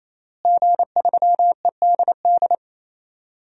Morse code, heard as a keyed tone, about 28 wpm, 700 Hz. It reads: G3EDD
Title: DXCC-100 Call Signs - 5 Characters - Repeated 3x - 28wpm